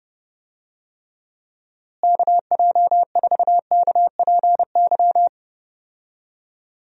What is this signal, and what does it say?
Morse code, heard as a keyed tone, about 32 wpm, 700 Hz. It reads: KJ4KPY
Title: USA Call Signs - 6 Characters - Repeated 2x - 30wpm